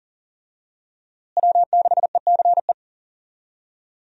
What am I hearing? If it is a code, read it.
Morse code, heard as a keyed tone, about 40 wpm, 700 Hz. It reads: W6ECE
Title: USA Call Signs - 5 Characters - Repeated 2x - 40wpm